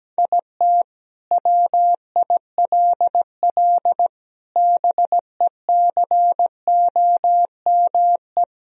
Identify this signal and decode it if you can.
Morse code, heard as a keyed tone, about 17 wpm, 700 Hz. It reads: IT WILL BECOME
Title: Sentences from Top 1000 Words - Review Part 3 - 17wpm